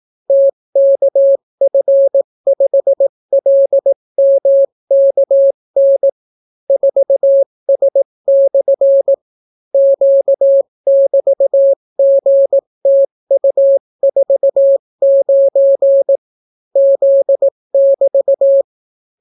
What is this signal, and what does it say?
Morse code, heard as a keyed tone, about 18 wpm, 550 Hz. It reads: TKF5LMKN 4S/ Q<BT>GTU49 Z<BT>